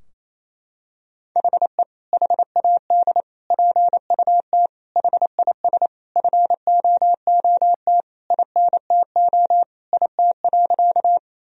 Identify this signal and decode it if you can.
Morse code, heard as a keyed tone, about 28 wpm, 700 Hz. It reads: HE HAD PUT HIS FOOT INTO IT.
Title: Sentences from Top 1000 Words - Review Part 3 - 28wpm